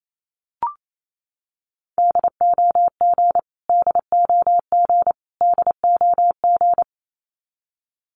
Morse code, heard as a keyed tone, about 28 wpm, 700 Hz. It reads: DOG DOG DOG
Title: Top 500 Words Repeated 28wpm